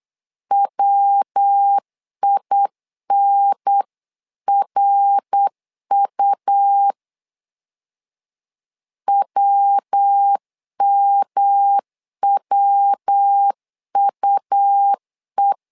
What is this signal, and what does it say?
Morse code, heard as a keyed tone, about 8 wpm, 800 Hz. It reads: WINRU WMWUE
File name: LEZIONI / 3 MWRO / 3 MWR 8wpm 2444